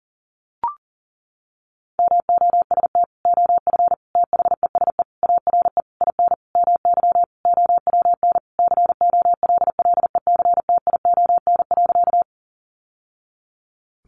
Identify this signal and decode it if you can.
Morse code, heard as a keyed tone, about 40 wpm, 700 Hz. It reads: MOST OF THESE ARE IN MY OWN COLLECTION.
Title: Sentences from Top 1200 Words 40wpm